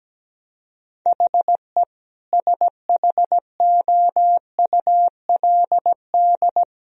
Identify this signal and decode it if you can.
Morse code, heard as a keyed tone, about 17 wpm, 700 Hz. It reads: HE SHOULD
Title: Sentences from Top 1000 Words - Review Part 3 - 17wpm